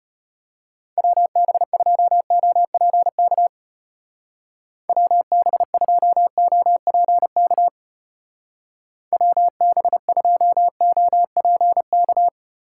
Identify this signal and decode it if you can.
Morse code, heard as a keyed tone, about 32 wpm, 700 Hz. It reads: WB2OPK WB2OPK WB2OPK